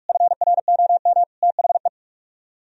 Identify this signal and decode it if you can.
Morse code, heard as a keyed tone, about 43 wpm, 700 Hz. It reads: FROM THE